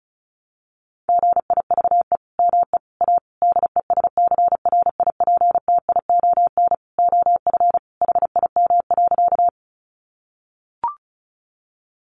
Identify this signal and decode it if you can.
Morse code, heard as a keyed tone, about 34 wpm, 700 Hz. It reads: GIVE ME A DESCRIPTION OF HIM.
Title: Sentences from Top 1300 Words 35wpm